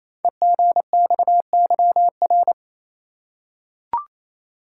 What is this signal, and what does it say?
Morse code, heard as a keyed tone, about 27 wpm, 700 Hz. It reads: EGXYR
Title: Code Groups - 5 Letters - 28wpm